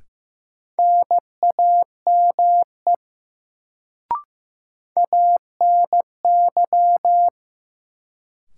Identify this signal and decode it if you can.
Morse code, heard as a keyed tone, about 15 wpm, 700 Hz. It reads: NAME ANY
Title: Sentences from Top 200 Words - New Word Drill - 15wpm